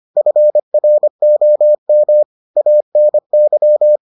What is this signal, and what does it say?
Morse code, heard as a keyed tone, about 25 wpm, 600 Hz. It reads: FROM ANY